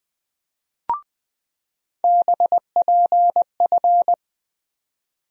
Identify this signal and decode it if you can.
Morse code, heard as a keyed tone, about 20 wpm, 700 Hz. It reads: BPF
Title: USA Call Sign Postfixes - 3 Letters 20wpm